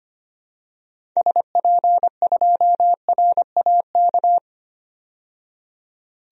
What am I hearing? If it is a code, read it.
Morse code, heard as a keyed tone, about 25 wpm, 700 Hz. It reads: SP2RAK
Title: DXCC-100 Call Signs - 6 Characters - 25wpm